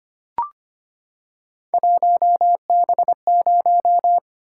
Morse code, heard as a keyed tone, about 25 wpm, 700 Hz. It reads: 1B0
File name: Three Letter-Number Combinations - 25wpm